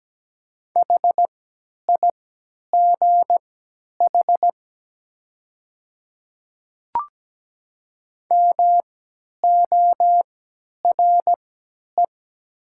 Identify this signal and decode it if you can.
Morse code, heard as a keyed tone, about 17 wpm, spaced as for 10 wpm, 700 Hz. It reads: HIGH MORE
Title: 3x Character Spacing - Top 300 Words - 17wpm